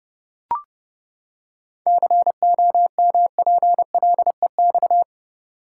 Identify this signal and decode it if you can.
Morse code, heard as a keyed tone, about 32 wpm, 700 Hz. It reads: COMPLEX
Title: Seven Letter Words - 30wpm